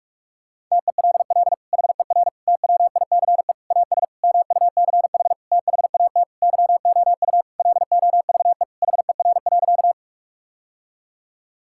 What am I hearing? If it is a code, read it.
Morse code, heard as a keyed tone, about 42 wpm, 700 Hz. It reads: TELL HER TWICE AS MUCH THAT YOU LOVE HER.
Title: Sentences from Top 1600 Words 45wpm